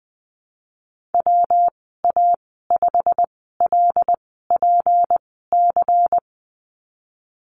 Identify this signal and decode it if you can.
Morse code, heard as a keyed tone, about 20 wpm, spaced as for 15 wpm, 700 Hz. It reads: WA5LPC
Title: 173 Lesson 28g - Number 9 - Call Signs 20wpm